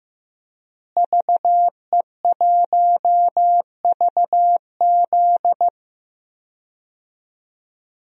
Morse code, heard as a keyed tone, about 15 wpm, 700 Hz. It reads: VE1VZ
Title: DXCC-100 Call Signs - 5 Characters - Repeated 3x - 15wpm